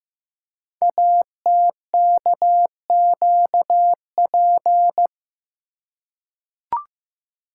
Morse code, heard as a keyed tone, about 15 wpm, 700 Hz. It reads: ATKQP
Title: Code Groups - 5 Letters - 15wpm